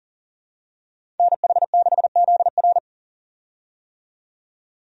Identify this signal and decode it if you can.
Morse code, heard as a keyed tone, about 40 wpm, 700 Hz. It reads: NH6ZR
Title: USA Call Signs - 5 Characters - 40wpm